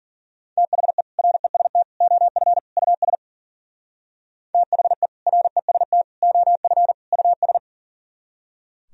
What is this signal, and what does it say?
Morse code, heard as a keyed tone, about 49 wpm, 700 Hz. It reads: THE REST OF US THE REST OF US
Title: Speed-Racing - Sets of 4 Words 40wpm